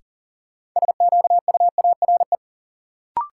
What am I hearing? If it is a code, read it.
Morse code, heard as a keyed tone, about 40 wpm, 700 Hz. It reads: SQUARE
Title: Sentences from Top 1000 Words - Review Part 7 - New Word Drill - 40wpm